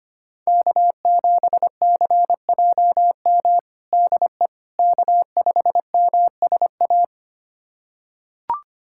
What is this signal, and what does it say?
Morse code, heard as a keyed tone, about 25 wpm, 700 Hz. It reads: K7CJM DE K5MSA
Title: USA Callsign DE Callsign - 5 Characters - 25wpm